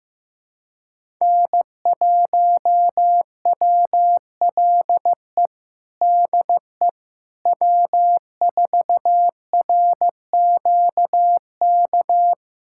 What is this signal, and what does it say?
Morse code, heard as a keyed tone, about 15 wpm, 700 Hz. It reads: N1WLE DE W4RQK
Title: USA Callsign DE Callsign - 5 Characters - 15wpm